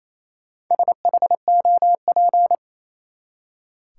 Morse code, heard as a keyed tone, about 28 wpm, 700 Hz. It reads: SHOP
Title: Four Letter Words - 28wpm